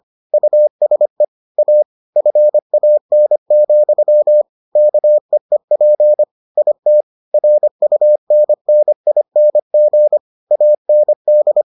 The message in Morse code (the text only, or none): USE A FAN, KEEP IT RUNNING AND